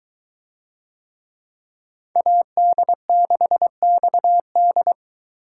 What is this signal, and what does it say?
Morse code, heard as a keyed tone, about 21 wpm, 700 Hz. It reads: AD6XD